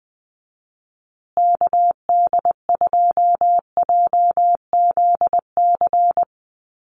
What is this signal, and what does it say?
Morse code, heard as a keyed tone, about 20 wpm, 700 Hz. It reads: KD2JZC
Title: USA Call Signs Repeated 2x 20wpm